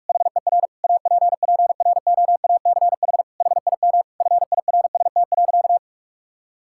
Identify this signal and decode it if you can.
Morse code, heard as a keyed tone, about 47 wpm, 700 Hz. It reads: HER APPROACH HIM FIRST.
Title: Sentences from Top 1000 Words - Review Part 6 - 45wpm